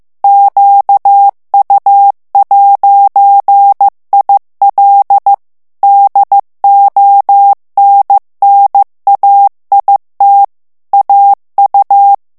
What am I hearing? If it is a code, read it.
Morse code, heard as a keyed tone, about 15 wpm, 800 Hz. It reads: QU'IL DONNAIT AU
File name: FRANCAIS TEXTE CLAIR 2 VITESSE 14 MOTS MINUTES